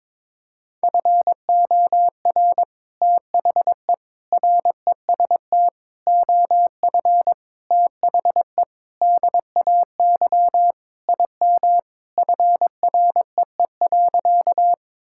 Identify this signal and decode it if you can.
Morse code, heard as a keyed tone, about 22 wpm, 700 Hz. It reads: FOR THE REST OF THE DAY IM FREE.
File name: practice / 22wpm / Sentences with 3-Word Phrases - 22wpm